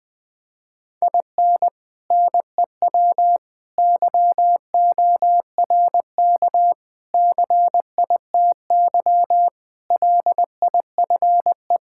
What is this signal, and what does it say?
Morse code, heard as a keyed tone, about 20 wpm, 700 Hz. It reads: IN NEW YORK CITY LIFE
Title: Sentences with 4-Word Phrases - Spoken Beginning - 20wpm